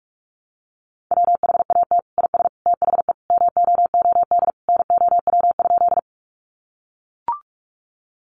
Morse code, heard as a keyed tone, about 44 wpm, 700 Hz. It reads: WHAT IS THE MOOD NOW?
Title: Sentences from Top 1100 Words 45wpm